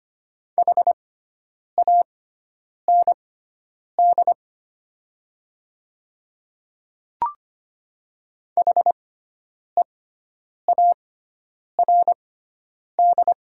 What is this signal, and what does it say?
Morse code, heard as a keyed tone, about 25 wpm, spaced as for 9 wpm, 700 Hz. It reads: HAND HEARD
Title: 6x Character Spacing - Top 500 Words - 25wpm